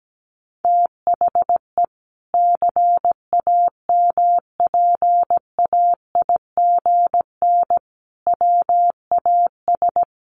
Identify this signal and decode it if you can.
Morse code, heard as a keyed tone, about 17 wpm, 700 Hz. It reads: THE CAMPAIGN WAS
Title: Sentences from Top 1200 Words 17wpm